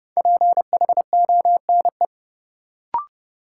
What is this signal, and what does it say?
Morse code, heard as a keyed tone, about 30 wpm, 700 Hz. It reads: PHONE
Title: Sentences from Top 1100 Words - New Word Drill - 30wpm